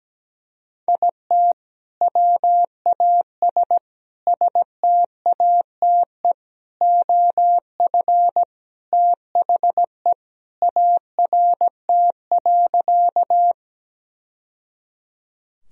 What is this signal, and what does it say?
Morse code, heard as a keyed tone, about 17 wpm, 700 Hz. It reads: IT WAS STATE OF THE ART.
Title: Sentences from Top 1000 Words - Review Part 3 - 17wpm